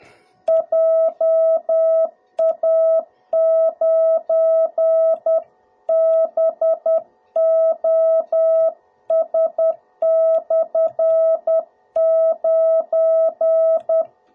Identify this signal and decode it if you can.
Morse code, heard as a keyed tone, about 10 wpm, 660 Hz. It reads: JA9BOS/9
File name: jh8cbh / 2021cw / cw02b